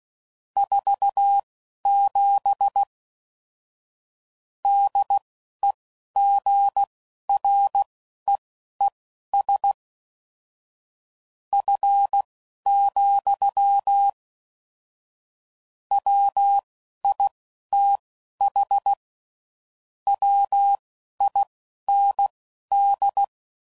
Morse code, heard as a keyed tone, about 16 wpm, spaced as for 11 wpm, 800 Hz. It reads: 47 DEGREES F, WITH WIND
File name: CW Code Groups @10WPM / Weather Sets / Weather 5 5